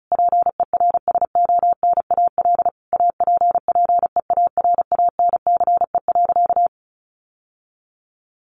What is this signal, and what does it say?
Morse code, heard as a keyed tone, about 34 wpm, 700 Hz. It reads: PERSONAL APPEARANCE.
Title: Sentences from Top 1100 Words 35wpm